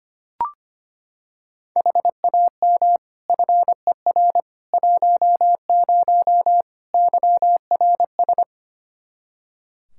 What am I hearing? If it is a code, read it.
Morse code, heard as a keyed tone, about 25 wpm, 700 Hz. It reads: HAM FER 10 YRS
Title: K1USN SST QSO Practice 25wpm